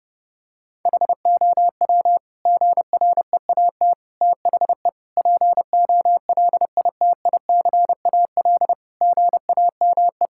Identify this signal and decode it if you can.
Morse code, heard as a keyed tone, about 30 wpm, 700 Hz. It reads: HOW GREAT THE POLITICAL GAME
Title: Sentences from Top 1000 Words - Review Part 5 - 30wpm